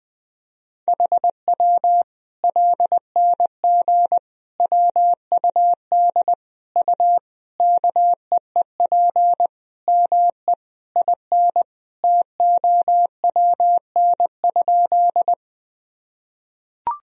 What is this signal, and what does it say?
Morse code, heard as a keyed tone, about 20 wpm, 700 Hz. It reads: HW LNG WUD U KEEP ME IN TOWN?
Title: Sentences from Top 500 Words and Abbreviations 20wpm